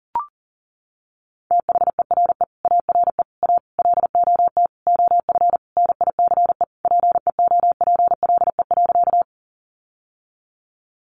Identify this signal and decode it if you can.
Morse code, heard as a keyed tone, about 40 wpm, 700 Hz. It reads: THERE ARE A LOT OF NICE PEOPLE.
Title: Sentences from Top 1100 Words 40wpm